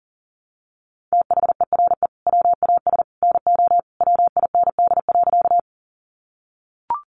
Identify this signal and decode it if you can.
Morse code, heard as a keyed tone, about 40 wpm, 700 Hz. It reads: THERE WAS NO WIND.
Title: Sentences from Top 400 Words 40wpm